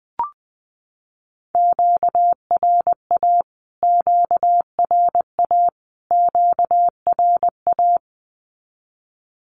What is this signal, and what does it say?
Morse code, heard as a keyed tone, about 20 wpm, 700 Hz. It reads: QRA QRA QRA
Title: Q-Codes Repeated 3x 20wpm